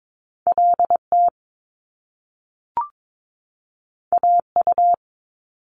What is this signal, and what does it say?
Morse code, heard as a keyed tone, about 22 wpm, 700 Hz. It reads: LT AU
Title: Top Two Letter Combinations in Words 22wpm